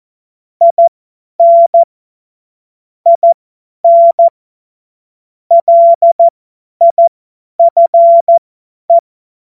Morse code, heard as a keyed tone, about 14 wpm, spaced as for 10 wpm, 680 Hz. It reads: IN IN LIFE
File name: Morsekurs / Witze / Witz 14. Dezember 2021, 50 F70